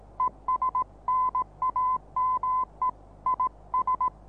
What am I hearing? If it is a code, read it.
Morse code, heard as a keyed tone, about 18 wpm, 1020 Hz. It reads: ES NAME IS
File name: repos / Morsekurs / DJ4UF Morsen26